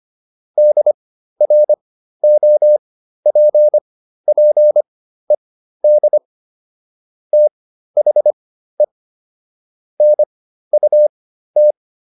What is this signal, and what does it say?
Morse code, heard as a keyed tone, about 25 wpm, spaced as for 13 wpm, 600 Hz. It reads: DROPPED THE NUT